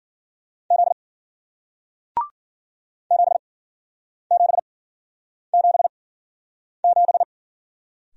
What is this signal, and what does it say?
Morse code, heard as a keyed tone, about 53 wpm, spaced as for 9 wpm, 700 Hz. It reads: Z 7777